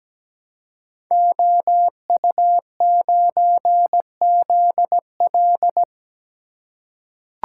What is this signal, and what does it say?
Morse code, heard as a keyed tone, about 17 wpm, 700 Hz. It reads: OU9ZL
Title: Code Groups - 5 Letter:Numbers - 17wpm